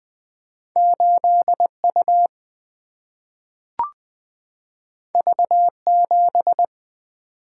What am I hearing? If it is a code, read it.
Morse code, heard as a keyed tone, about 20 wpm, 700 Hz. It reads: 8 U V 7
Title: Single-Letter Single-Number 20wpm